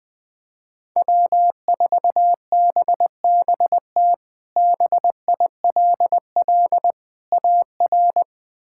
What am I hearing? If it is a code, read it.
Morse code, heard as a keyed tone, about 20 wpm, 700 Hz. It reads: W4BBT BILL AR
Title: K1USN SST QSO Practice - 20wpm